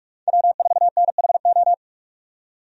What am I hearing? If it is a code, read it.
Morse code, heard as a keyed tone, about 47 wpm, 700 Hz. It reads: W4NHO
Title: USA Call Signs - 5 Characters - 45wpm